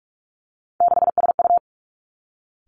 Morse code, heard as a keyed tone, about 46 wpm, 700 Hz. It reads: 6SU
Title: Three Challenging Letter-Number Combinations 45wpm